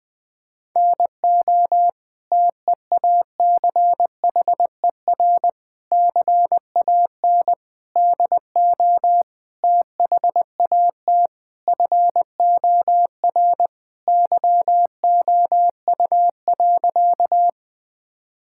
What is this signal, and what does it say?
Morse code, heard as a keyed tone, about 20 wpm, 700 Hz. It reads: NO TEACHER CAN DO THAT FOR YOU.
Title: Sentences from Top 1000 Words - Review Part 6 - 20wpm